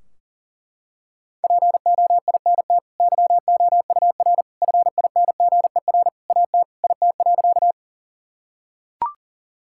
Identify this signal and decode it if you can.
Morse code, heard as a keyed tone, about 40 wpm, 700 Hz. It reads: POINT YOUR FINGER AT IT.
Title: Sentences from Top 1600 Words 40wpm